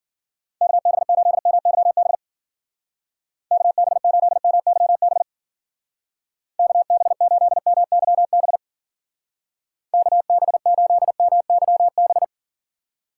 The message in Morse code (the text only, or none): KB8MYB KB8MYB KB8MYB KB8MYB